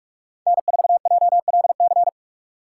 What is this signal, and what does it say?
Morse code, heard as a keyed tone, about 42 wpm, 700 Hz. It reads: N4JLC
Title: USA Call Signs - 5 Characters - 2x Character Spacing - 45wpm